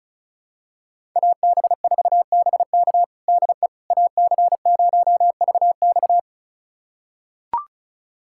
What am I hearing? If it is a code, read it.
Morse code, heard as a keyed tone, about 34 wpm, 700 Hz. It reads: AB4BK DE AC0VX
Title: USA Callsign DE Callsign - 5 Characters - 35wpm